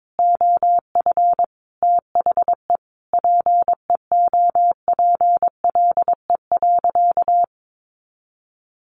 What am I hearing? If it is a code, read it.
Morse code, heard as a keyed tone, about 22 wpm, 700 Hz. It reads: OF THE PEOPLE.
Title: Sentences from Top 1500 Words 22wpm